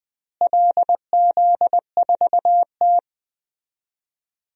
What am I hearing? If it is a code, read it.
Morse code, heard as a keyed tone, about 20 wpm, 700 Hz. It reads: LZ4T
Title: DXCC-100 Call Signs - 4 Characters - Repeated 3x - 20wpm